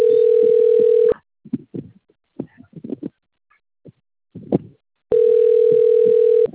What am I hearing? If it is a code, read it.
Ringback tone (North America), 440 + 480 Hz.
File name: out- - - - - .20133